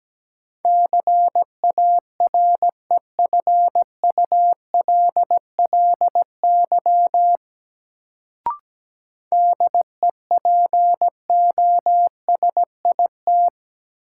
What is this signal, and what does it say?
Morse code, heard as a keyed tone, about 17 wpm, 700 Hz. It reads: CAREFULLY DEPOSIT
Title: Sentences from Top 1300 Words - New Word Drill Alt - 17wpm